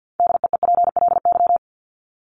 Morse code, heard as a keyed tone, about 50 wpm, 700 Hz. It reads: DEEPLY
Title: Top 1751-2000 Words 50wpm